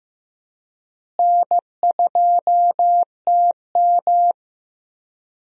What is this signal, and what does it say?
Morse code, heard as a keyed tone, about 15 wpm, 700 Hz. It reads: N2TM
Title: USA Call Signs - 4 Characters - Repeated 3x - 15wpm